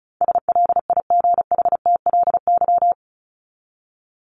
Morse code, heard as a keyed tone, about 34 wpm, 700 Hz. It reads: SLIGHTLY
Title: Top 1251-1500 Words 35wpm